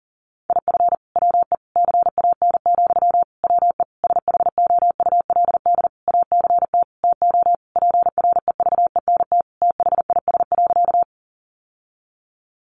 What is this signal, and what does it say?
Morse code, heard as a keyed tone, about 40 wpm, 700 Hz. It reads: IF WE CAN, WE SHOULD ACT TO PREVENT THIS.
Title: Sentences from Top 1100 Words 40wpm